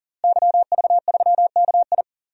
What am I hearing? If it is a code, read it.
Morse code, heard as a keyed tone, about 39 wpm, 700 Hz. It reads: YV3KI
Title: DXCC-100 Call Signs - 5 Characters - Repeated 2x - 40wpm